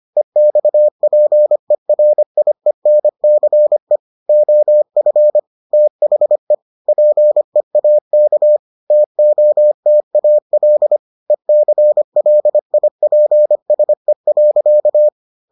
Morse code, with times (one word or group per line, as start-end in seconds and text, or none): 0.2-4.0 EXPERIENCE
4.3-5.4 OF
5.7-6.5 THE
6.9-8.6 PEAK
8.9-11.0 TOTAL
11.3-15.1 ECLIPSE.